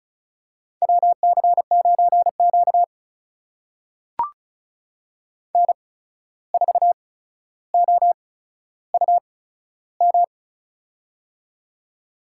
Morse code, heard as a keyed tone, about 36 wpm, spaced as for 14 wpm, 700 Hz. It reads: WC9Q N4OUM
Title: USA Call Signs - 8x Character Spacing - 35wpm